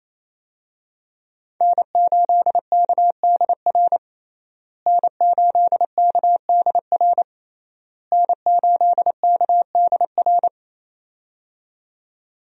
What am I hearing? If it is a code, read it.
Morse code, heard as a keyed tone, about 28 wpm, 700 Hz. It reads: N8KDR N8KDR N8KDR
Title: USA Call Signs - 5 Characters - Repeated 3x - 28wpm